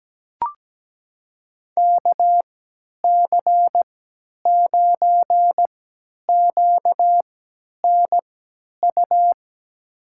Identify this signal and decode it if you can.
Morse code, heard as a keyed tone, about 17 wpm, spaced as for 10 wpm, 700 Hz. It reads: KC9QNU
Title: USA Call Signs - 6 Characters - 3x Character Spacing - 17wpm